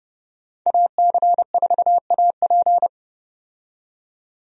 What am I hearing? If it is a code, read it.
Morse code, heard as a keyed tone, about 30 wpm, 700 Hz. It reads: AC4AP
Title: USA Call Signs - 5 Characters - 30wpm